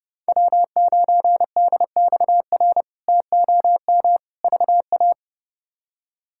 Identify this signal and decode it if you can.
Morse code, heard as a keyed tone, about 32 wpm, 700 Hz. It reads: W9DXR TOM VA
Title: K1USN SST QSO Practice - 30wpm